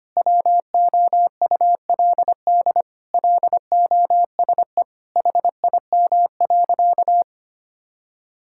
Morse code, heard as a keyed tone, about 25 wpm, 700 Hz. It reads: WOULD LOSE HIM.
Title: Sentences from Top 1600 Words 25wpm